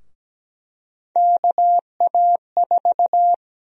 Morse code, heard as a keyed tone, about 17 wpm, 700 Hz. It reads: KA4
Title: USA Call Sign Prefixes - Repeated 2x - 17wpm